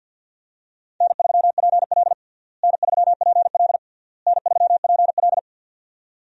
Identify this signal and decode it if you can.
Morse code, heard as a keyed tone, about 46 wpm, 700 Hz. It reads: N3PL N3PL N3PL
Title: USA Call Signs - 4 Characters - Repeated 3x - 50wpm